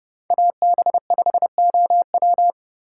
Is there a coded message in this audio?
Yes: Morse code from 0.3 to 2.5 s, heard as a keyed tone, about 29 wpm, 700 Hz. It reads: AB5OW